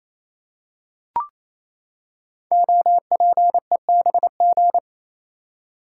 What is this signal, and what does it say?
Morse code, heard as a keyed tone, about 28 wpm, 700 Hz. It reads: OPEBG
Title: Code Groups - 5 Letter:Numbers - 28wpm